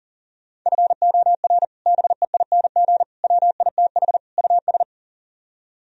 Morse code, heard as a keyed tone, about 52 wpm, 700 Hz. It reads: FOR BEING WITH US